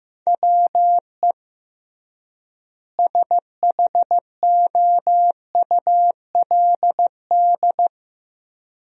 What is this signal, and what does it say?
Morse code, heard as a keyed tone, about 15 wpm, 700 Hz. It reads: WE SHOULD
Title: 3x Word-Spacing - Sentences from Top 300 Words - 15wpm